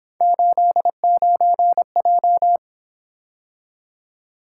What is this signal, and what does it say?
Morse code, heard as a keyed tone, about 27 wpm, 700 Hz. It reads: 89J
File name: Speed-Racing - Three Letter-Number Combinations - 22wpm